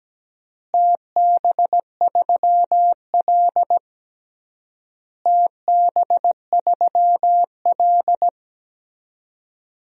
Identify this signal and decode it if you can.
Morse code, heard as a keyed tone, about 17 wpm, 700 Hz. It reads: TB3L TB3L
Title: DXCC-100 Call Signs - 4 Characters - Repeated 3x - 17wpm